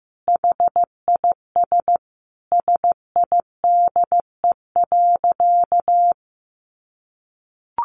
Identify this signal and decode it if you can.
Morse code, heard as a keyed tone, about 15 wpm, 700 Hz. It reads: HIS SIDE.
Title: A Sound Of Thunder 15wpm